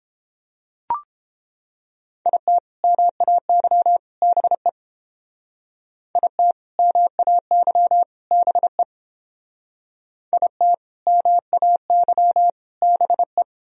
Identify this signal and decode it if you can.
Morse code, heard as a keyed tone, about 29 wpm, 700 Hz. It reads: IT MAY BE IT MAY BE IT MAY BE